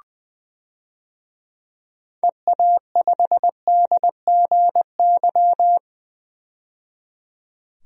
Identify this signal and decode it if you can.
Morse code, heard as a keyed tone, about 21 wpm, 700 Hz. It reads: EA5DGY
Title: DXCC-100 Call Signs - 6 Characters - 20wpm